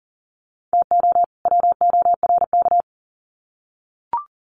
Morse code, heard as a keyed tone, about 40 wpm, 700 Hz. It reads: TO WORK
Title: Sets of 2 Words 40wpm